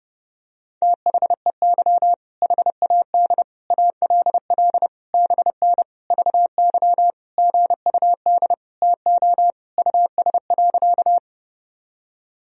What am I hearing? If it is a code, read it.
Morse code, heard as a keyed tone, about 30 wpm, 700 Hz. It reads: THEY HAD ALL BN VY GUD TO US.
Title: Sentences from Top 200 Words and Abbreviations 30wpm